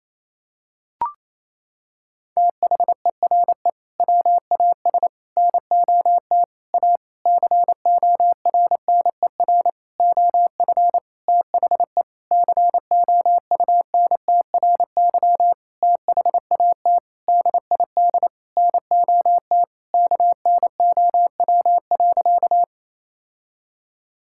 Morse code, heard as a keyed tone, about 28 wpm, 700 Hz. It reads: THERE WAS NOT A CORNER OF THE COUNTRY THAT DID NOT KNOW.
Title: Sentences from Top 1600 Words 28wpm